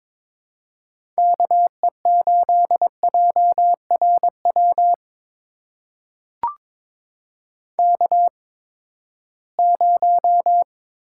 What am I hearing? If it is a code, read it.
Morse code, heard as a keyed tone, about 22 wpm, spaced as for 15 wpm, 700 Hz. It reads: KE8JRW K0
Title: USA Call Signs - 8x Character Spacing - 22wpm